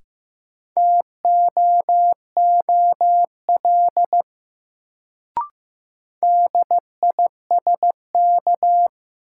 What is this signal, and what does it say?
Morse code, heard as a keyed tone, about 15 wpm, 700 Hz. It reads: TOOL DISK
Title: Sentences from Top 1000 Words - Review Part 6 - New Word Drill - 15wpm